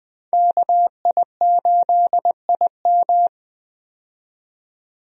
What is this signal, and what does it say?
Morse code, heard as a keyed tone, about 20 wpm, 700 Hz. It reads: KI8IM
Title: USA Call Signs - 5 Characters - 20wpm